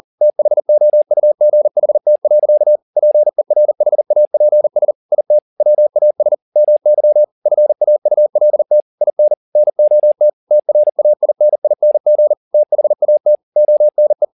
Morse code, heard as a keyed tone, about 40 wpm, 600 Hz. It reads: THOUGHT. PERHAPS IT WAS MY FAULT IN NOT TRAINING THAT ONE
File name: the old man and the sea 2 40wpm